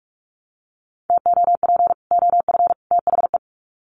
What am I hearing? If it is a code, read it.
Morse code, heard as a keyed tone, about 44 wpm, 700 Hz. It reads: TOP OF THE
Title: Sets of 3 Words 45wpm